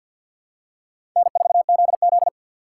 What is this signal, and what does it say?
Morse code, heard as a keyed tone, about 53 wpm, 690 Hz. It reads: N4ZZ
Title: USA Call Signs - 4 Characters - 6x Character Spacing - 50wpm